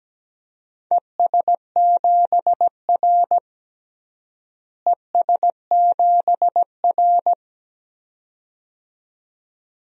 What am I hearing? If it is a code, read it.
Morse code, heard as a keyed tone, about 17 wpm, 700 Hz. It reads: ES7R ES7R
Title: DXCC-100 Call Signs - 4 Characters - Repeated 2x - 17wpm